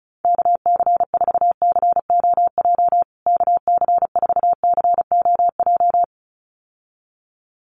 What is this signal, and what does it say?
Morse code, heard as a keyed tone, about 34 wpm, 700 Hz. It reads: KC4COJ KC4COJ
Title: USA Call Signs Repeated 2x 35wpm